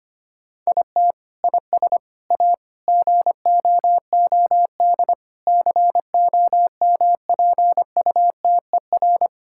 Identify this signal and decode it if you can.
Morse code, heard as a keyed tone, about 25 wpm, 700 Hz. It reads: IT IS A GOOD COMPUTER
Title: Sentences from Top 1000 Words - Review Part 5 - 25wpm